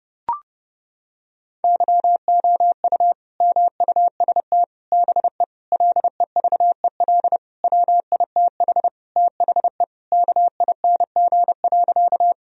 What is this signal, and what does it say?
Morse code, heard as a keyed tone, about 30 wpm, 700 Hz. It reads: YOU MUST BE LEVEL WITH THE KING.
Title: Sentences from Top 1000 Words - Review Part 4 - 30wpm